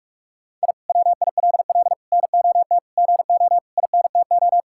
Morse code, heard as a keyed tone, about 46 wpm, 700 Hz. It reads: I WILL NOT GO INTO
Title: Sentences from Top 1000 Words - Review Part 6 - 45wpm